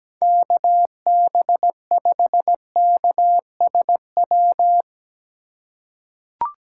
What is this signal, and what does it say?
Morse code, heard as a keyed tone, about 17 wpm, 700 Hz. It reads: KB5KSW
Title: USA Call Signs - 2x Character Spacing - 17wpm